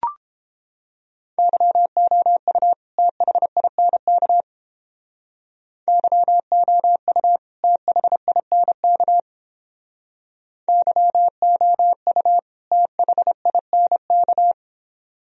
Morse code, heard as a keyed tone, about 31 wpm, 700 Hz. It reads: YOU THINK YOU THINK YOU THINK